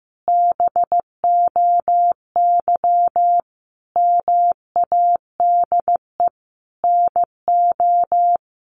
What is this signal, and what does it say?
Morse code, heard as a keyed tone, about 15 wpm, 700 Hz. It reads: BOY MADE NO
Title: Sentences from Top 1400 Words 15wpm